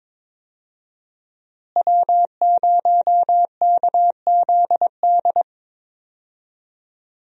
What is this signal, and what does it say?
Morse code, heard as a keyed tone, about 21 wpm, 700 Hz. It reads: W0KZD
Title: USA Call Signs - 5 Characters - Repeated 2x - 22wpm